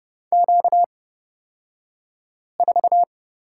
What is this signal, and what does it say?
Morse code, heard as a keyed tone, about 30 wpm, spaced as for 5 wpm, 700 Hz. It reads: Q4
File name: Lesson 38d - Letter q - Two Characters 30wpm